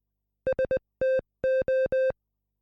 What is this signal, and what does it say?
Morse code, heard as a keyed tone, about 20 wpm, 520 Hz. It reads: STO